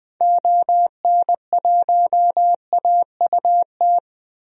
Morse code, heard as a keyed tone, about 20 wpm, 700 Hz. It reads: ON1AUT
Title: DXCC-100 Call Signs - 6 Characters - 20wpm